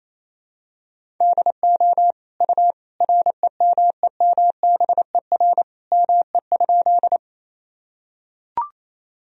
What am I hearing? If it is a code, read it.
Morse code, heard as a keyed tone, about 27 wpm, 700 Hz. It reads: DO U REMEMBER ME?
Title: Sentences from Top 400 Words and Abbreviations 28wpm